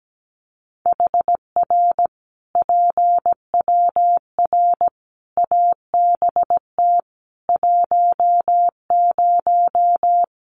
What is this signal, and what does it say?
Morse code, heard as a keyed tone, about 17 wpm, 700 Hz. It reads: HR PWR ABT 10
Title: Common QSO Phrases 17wpm